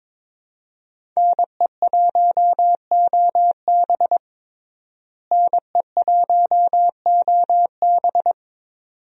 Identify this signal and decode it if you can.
Morse code, heard as a keyed tone, about 22 wpm, 700 Hz. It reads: NE1OB NE1OB
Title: USA Call Signs - 5 Characters - Repeated 2x - 22wpm